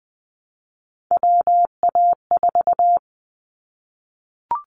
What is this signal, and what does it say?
Morse code, heard as a keyed tone, about 20 wpm, 700 Hz. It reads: WA4
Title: Callsign Prefixes for Top 25 DXCC - Repeated 3x - 20wpm